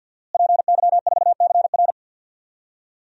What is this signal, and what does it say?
Morse code, heard as a keyed tone, about 53 wpm, 700 Hz. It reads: PY4CR
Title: DXCC-100 Call Signs - 5 Characters - Repeated 2x - 50wpm